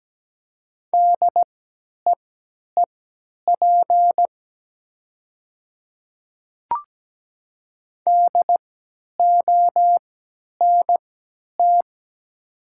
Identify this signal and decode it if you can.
Morse code, heard as a keyed tone, about 17 wpm, spaced as for 10 wpm, 700 Hz. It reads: DEEP DONT
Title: 3x Character Spacing - Top 500 Words - 17wpm